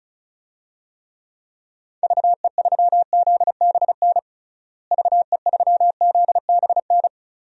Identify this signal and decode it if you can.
Morse code, heard as a keyed tone, about 36 wpm, 700 Hz. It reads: VE3ZBN VE3ZBN
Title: DXCC-100 Call Signs - 6 Characters - Repeated 2x - 35wpm